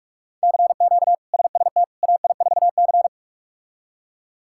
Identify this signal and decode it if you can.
Morse code, heard as a keyed tone, about 46 wpm, 700 Hz. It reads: CQ SST AI4C
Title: K1USN SST QSO Practice - 45wpm